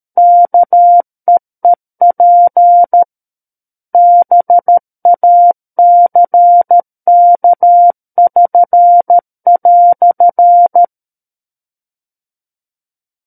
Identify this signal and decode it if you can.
Morse code, heard as a keyed tone, about 13 wpm, 700 Hz. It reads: KEEP BACK<SN>"
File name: Book One / 04-Chapter 4 - 13 WPM